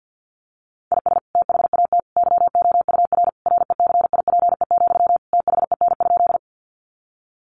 Morse code, heard as a keyed tone, about 50 wpm, 700 Hz. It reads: IS THAT YOUR RECIPE, THEN?
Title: Sentences from Top 1200 Words 50wpm